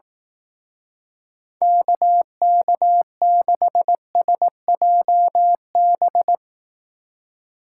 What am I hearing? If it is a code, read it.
Morse code, heard as a keyed tone, about 18 wpm, 700 Hz. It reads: KK6SJB
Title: Speed Racing - USA 2x3 Call Signs 15wpm